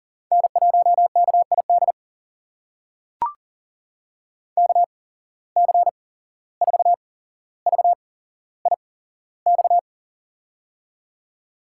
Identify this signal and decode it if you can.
Morse code, heard as a keyed tone, about 40 wpm, spaced as for 16 wpm, 700 Hz. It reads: N1KID KC4VIX
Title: USA Call Signs - 8x Character Spacing - 40wpm